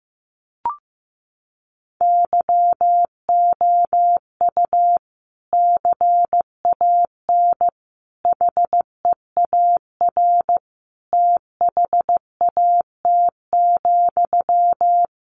Morse code, heard as a keyed tone, about 15 wpm, 700 Hz. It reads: YOU CAN HEAR THAT,
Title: Sentences from Top 500 Words 15wpm